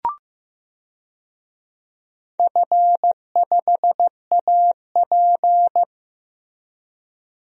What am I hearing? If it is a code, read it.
Morse code, heard as a keyed tone, about 15 wpm, 700 Hz. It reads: F5AP
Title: DXCC-100 Call Signs - 4 Characters - Repeated 3x - 15wpm